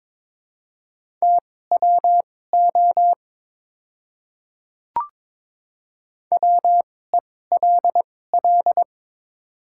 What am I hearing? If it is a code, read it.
Morse code, heard as a keyed tone, about 22 wpm, spaced as for 16 wpm, 700 Hz. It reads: TWO WELL
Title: 2x Character Spacing - Top 300 Words - 22wpm